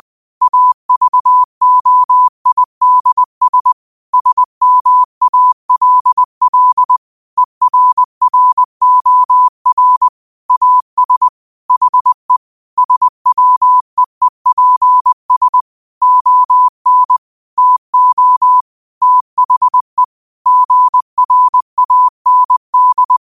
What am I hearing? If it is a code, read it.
Morse code, heard as a keyed tone, about 20 wpm, 1000 Hz. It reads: AVOIDS SMALL ERROR AS HE SWEEPS ON TO THE GRAND
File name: qotdfiles / 20wpm